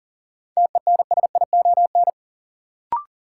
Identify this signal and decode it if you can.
Morse code, heard as a keyed tone, about 40 wpm, 700 Hz. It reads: TENSION
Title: Sentences from Top 1600 Words - New Word Drill Alt - 40wpm